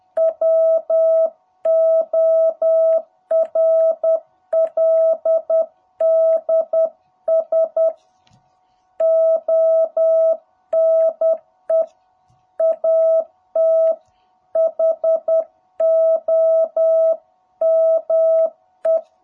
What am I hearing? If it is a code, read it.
Morse code, heard as a keyed tone, about 10 wpm, 660 Hz. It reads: WORLDS ONE AT HOME
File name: jh8cbh / 2021cw